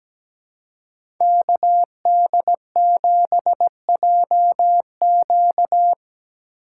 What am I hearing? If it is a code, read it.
Morse code, heard as a keyed tone, about 17 wpm, 700 Hz. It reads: KD7JQ
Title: USA Call Signs - 5 Characters - 2x Character Spacing - 17wpm